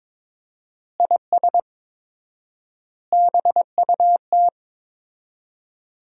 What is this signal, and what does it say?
Morse code, heard as a keyed tone, about 22 wpm, 700 Hz. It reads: IS BUT
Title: 4x Word-Spacing - Sentences from Top 100 Words - 22wpm